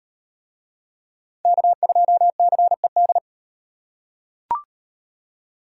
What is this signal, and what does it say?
Morse code, heard as a keyed tone, about 35 wpm, 700 Hz. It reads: K2CED